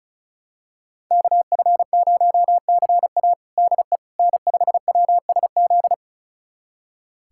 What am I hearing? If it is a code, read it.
Morse code, heard as a keyed tone, about 36 wpm, 700 Hz. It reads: KF0CA DE N5WSZ
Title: USA Callsign DE Callsign - 5 Characters - 35wpm